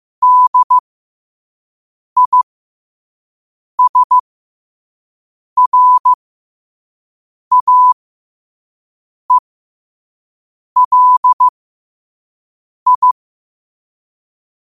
Morse code, heard as a keyed tone, about 15 wpm, spaced as for 5 wpm, 1000 Hz. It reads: DISRAELI
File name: qotdfiles / 5wpm